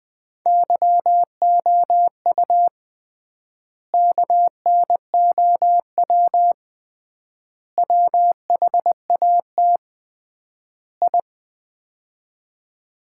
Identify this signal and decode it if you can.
Morse code, heard as a keyed tone, about 20 wpm, 700 Hz. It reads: YOU KNOW WHAT I
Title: 3x Word-Spacing - Sets of 4 Words - 20wpm